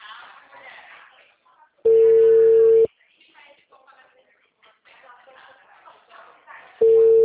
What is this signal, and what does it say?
Ringback tone (China, 450 Hz), with speech.
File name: out- -2020- - - .362